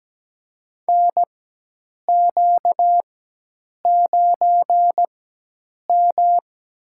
Morse code, heard as a keyed tone, about 17 wpm, spaced as for 8 wpm, 700 Hz. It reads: NQ9M
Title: USA Call Signs - 4 Characters - 4x Character Spacing - 17wpm